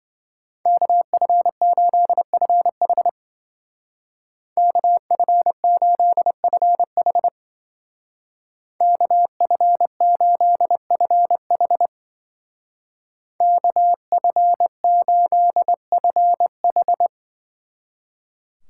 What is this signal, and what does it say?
Morse code, heard as a keyed tone, about 25 wpm, 700 Hz. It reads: KF8FH KF8FH KF8FH KF8FH